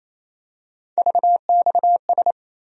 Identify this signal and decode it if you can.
Morse code, heard as a keyed tone, about 28 wpm, 700 Hz. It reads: VXS
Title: USA Call Sign Postfixes - 3 Letters 28wpm